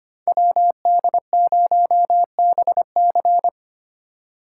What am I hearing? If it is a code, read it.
Morse code, heard as a keyed tone, about 32 wpm, 700 Hz. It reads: WD0BC